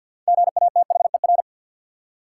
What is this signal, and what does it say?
Morse code, heard as a keyed tone, about 47 wpm, 700 Hz. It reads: GATHER
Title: Six Letter Words - 50wpm